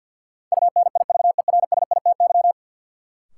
Morse code, heard as a keyed tone, about 53 wpm, 700 Hz. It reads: UNIVERSITY
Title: Sentences from Top 1300 Words - New Word Drill Alt - 50wpm